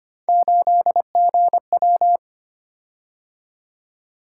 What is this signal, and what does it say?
Morse code, heard as a keyed tone, about 25 wpm, 700 Hz. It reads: 8GW
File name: Three Letter-Number Combinations - 25wpm